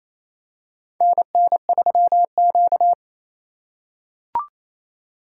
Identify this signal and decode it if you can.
Morse code, heard as a keyed tone, about 28 wpm, 700 Hz. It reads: NN3Q
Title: USA Call Signs - 4x Character Spacing - 28wpm